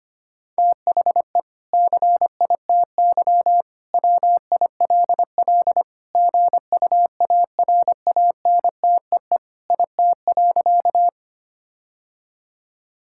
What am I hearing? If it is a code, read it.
Morse code, heard as a keyed tone, about 25 wpm, 700 Hz. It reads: THE CITY WILL GUARANTEE IT.
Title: Sentences from Top 1600 Words 25wpm